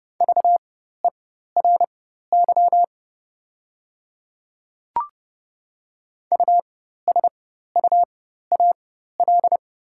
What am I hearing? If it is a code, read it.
Morse code, heard as a keyed tone, about 30 wpm, spaced as for 14 wpm, 690 Hz. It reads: VERY USUAL
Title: 4x Character Spacing - Top 500 Words - 30wpm